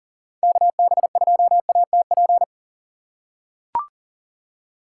Morse code, heard as a keyed tone, about 40 wpm, 700 Hz. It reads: KB2ATP
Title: USA Call Signs - 6 Characters - 3x Character Spacing - 40wpm